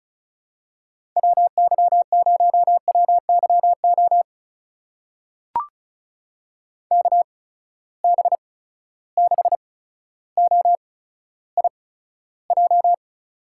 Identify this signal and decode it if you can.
Morse code, heard as a keyed tone, about 36 wpm, spaced as for 15 wpm, 700 Hz. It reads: WY0WYO KB6OIJ
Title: USA Call Signs - 6 Characters - 8x Character Spacing - 35wpm